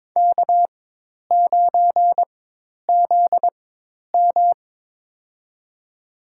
Morse code, heard as a keyed tone, about 22 wpm, spaced as for 10 wpm, 700 Hz. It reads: K9ZM
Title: USA Call Signs - 4 Characters - 4x Character Spacing - 22wpm